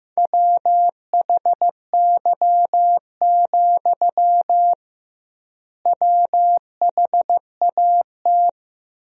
Morse code, heard as a keyed tone, about 16 wpm, 700 Hz. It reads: WHY, WHAT
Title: 2x Word-Spacing - Sentences from Top 200 Words - 15wpm